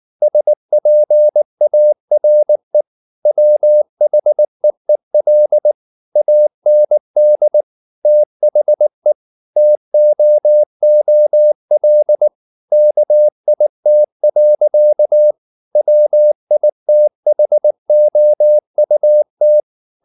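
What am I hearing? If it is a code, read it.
Morse code, heard as a keyed tone, about 19 wpm, 600 Hz. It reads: SPARE WHEEL AND THE TOOL KIT. WITHOUT
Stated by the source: Morse code at 19 wpm from CW med Gnister, reading Ken Follett's novel Never in English.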